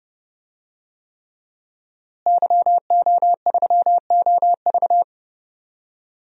Morse code, heard as a keyed tone, about 31 wpm, 700 Hz. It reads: YO3OV
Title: DXCC-100 Call Signs - 5 Characters - Repeated 3x - 30wpm